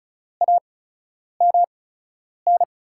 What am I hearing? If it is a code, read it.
Morse code, heard as a keyed tone, about 36 wpm, spaced as for 10 wpm, 700 Hz. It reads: AMN
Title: USA Call Signs - 5 Characters - 8x Character Spacing - 35wpm